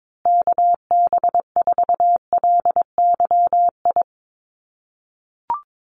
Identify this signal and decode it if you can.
Morse code, heard as a keyed tone, about 22 wpm, 700 Hz. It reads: KB4LYI
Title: USA Call Signs Repeated 2x 22wpm